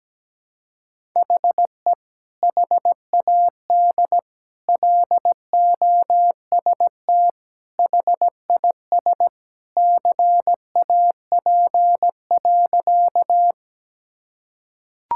Morse code, heard as a keyed tone, about 17 wpm, 700 Hz. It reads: HE HAD LOST HIS CAP.
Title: Sentences from Top 1600 Words 17wpm